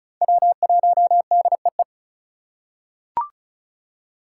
Morse code, heard as a keyed tone, about 35 wpm, 700 Hz. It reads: W1DEE
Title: USA Call Signs - 5 Characters - 3x Character Spacing - 35wpm